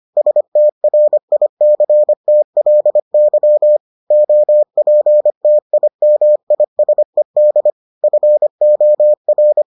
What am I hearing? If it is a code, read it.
Morse code, heard as a keyed tone, about 25 wpm, 600 Hz. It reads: STRICTLY OPTIMISED FOR